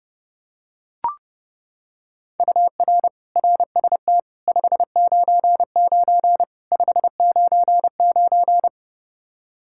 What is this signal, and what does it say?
Morse code, heard as a keyed tone, about 32 wpm, 700 Hz. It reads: UR RST 599 599
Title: Practice QSO - Exchange 1 - Beginning - 30wpm